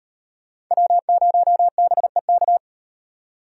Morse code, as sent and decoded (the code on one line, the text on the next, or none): .-- ----- -... -.-
W0BEK